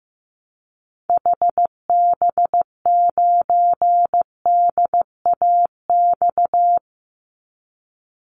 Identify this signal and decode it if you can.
Morse code, heard as a keyed tone, about 15 wpm, 700 Hz. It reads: HB9DAX
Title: Call Signs Repeated 2x - 15wpm